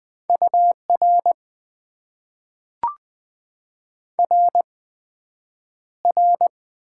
Morse code, heard as a keyed tone, about 20 wpm, spaced as for 9 wpm, 700 Hz. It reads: UR RR
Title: Top 90 QSO Elements - 8x Character Spacing - 20wpm